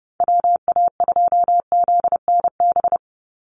Morse code, heard as a keyed tone, about 30 wpm, 700 Hz. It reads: WA2ZNB
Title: USA Call Signs 30wpm